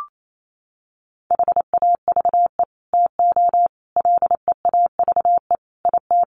Morse code, heard as a keyed tone, about 28 wpm, 700 Hz. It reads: HAVE TO LEAVE IT
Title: Sets of 5 Words Encore2 28wpm